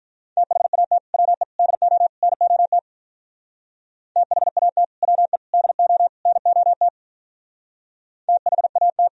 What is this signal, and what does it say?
Morse code, heard as a keyed tone, about 45 wpm, 700 Hz. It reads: THAT WE DO NOT THAT WE DO NOT THAT